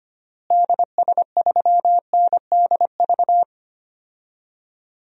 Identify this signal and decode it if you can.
Morse code, heard as a keyed tone, about 25 wpm, 700 Hz. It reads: DS3NDV
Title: DXCC-100 Call Signs - 6 Characters - Repeated 2x - 25wpm